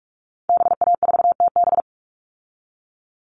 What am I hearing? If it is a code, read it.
Morse code, heard as a keyed tone, about 44 wpm, 700 Hz. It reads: BA4TB
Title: Call Signs Repeated 3x - 45wpm